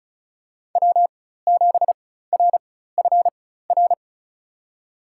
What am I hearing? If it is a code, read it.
Morse code, heard as a keyed tone, about 34 wpm, spaced as for 16 wpm, 700 Hz. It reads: W7RFR
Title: USA Call Signs - 5 Characters - 4x Character Spacing - 35wpm